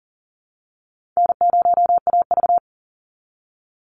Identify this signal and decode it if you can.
Morse code, heard as a keyed tone, about 39 wpm, 700 Hz. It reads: N0AV
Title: Easy Call Signs Repeated 2x - 40wpm